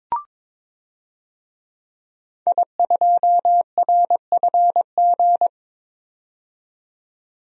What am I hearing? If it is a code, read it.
Morse code, heard as a keyed tone, about 21 wpm, 700 Hz. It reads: I2RFG
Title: DXCC-100 Call Signs - 5 Characters - 22wpm